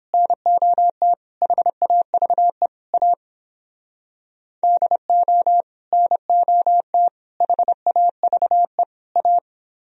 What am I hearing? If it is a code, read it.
Morse code, heard as a keyed tone, about 29 wpm, 700 Hz. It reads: NOT HAVE A DO NOT HAVE A